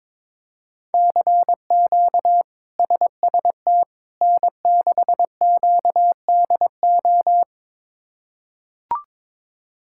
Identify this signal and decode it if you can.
Morse code, heard as a keyed tone, about 21 wpm, 700 Hz. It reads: CQ SST N6QDO
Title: K1USN SST QSO Practice - 22wpm